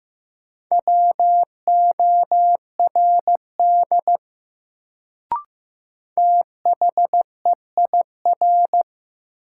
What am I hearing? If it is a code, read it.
Morse code, heard as a keyed tone, about 15 wpm, 700 Hz. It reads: WORD THEIR
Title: Sentences from Top 300 Words - New Word Drill - 15wpm